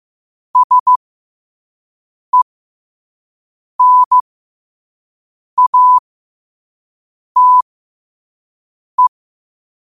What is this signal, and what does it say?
Morse code, heard as a keyed tone, about 15 wpm, spaced as for 5 wpm, 1000 Hz. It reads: SENATE
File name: qotdfiles / 5wpm